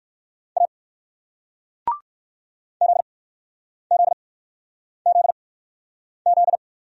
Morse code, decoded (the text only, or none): A ZZZZ